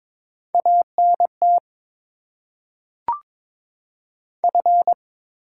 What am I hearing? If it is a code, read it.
Morse code, heard as a keyed tone, about 22 wpm, 700 Hz. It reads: ANT F
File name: Top 20 QSO Elements - 8x Character Spacing - 22wpm